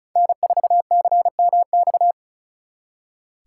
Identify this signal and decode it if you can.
Morse code, heard as a keyed tone, about 42 wpm, 700 Hz. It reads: N4CMX
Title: Speed Racing - USA 1x3 Call Signs 35wpm